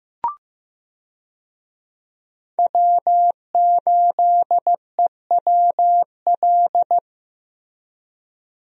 Morse code, heard as a keyed tone, about 15 wpm, 700 Hz. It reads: W8EWL
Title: USA Call Signs - 5 Characters - Repeated 3x - 15wpm